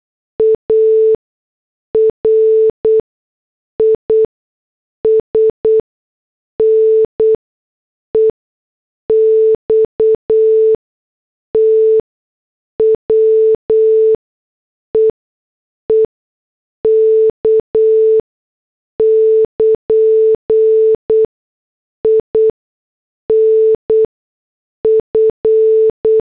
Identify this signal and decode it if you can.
Morse code, heard as a keyed tone, about 8 wpm, spaced as for 6 wpm, 440 Hz. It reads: ARIS NEXT WEEK <KN> INF